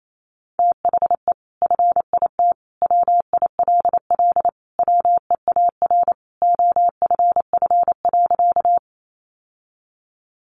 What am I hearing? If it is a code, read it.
Morse code, heard as a keyed tone, about 28 wpm, 700 Hz. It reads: THE FIT WILL WEAR OFF.
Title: Sentences from Top 1500 Words 28wpm